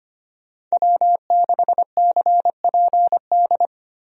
Morse code, heard as a keyed tone, about 25 wpm, 700 Hz. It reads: W6CPD
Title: USA Call Signs - 5 Characters - Repeated 3x - 25wpm